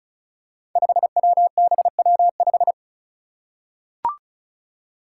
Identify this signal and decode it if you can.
Morse code, heard as a keyed tone, about 36 wpm, 700 Hz. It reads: 5WBW5
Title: Five Challenging Letter-Number Combinations 35wpm